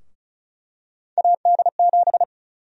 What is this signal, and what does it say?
Morse code, heard as a keyed tone, about 36 wpm, 700 Hz. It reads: AD7
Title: USA Call Sign Prefixes - Repeated 2x - 35wpm